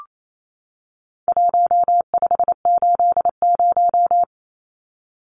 Morse code, heard as a keyed tone, about 28 wpm, 700 Hz. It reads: 1580
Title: Numbers - Contest Style - 4-Digit 28wpm